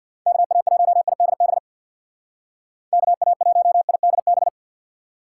Morse code, heard as a keyed tone, about 59 wpm, 700 Hz. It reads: KA1IDB KA1IDB
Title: Speed Racing - USA Call Signs 50wpm